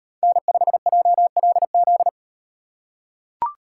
Morse code, heard as a keyed tone, about 38 wpm, 700 Hz. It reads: N5JLZ